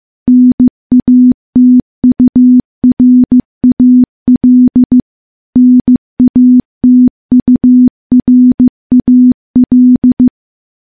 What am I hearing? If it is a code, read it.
Morse code, heard as a keyed tone, about 15 wpm, 250 Hz. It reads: NATURAL NATURAL
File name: morse-training-15WPM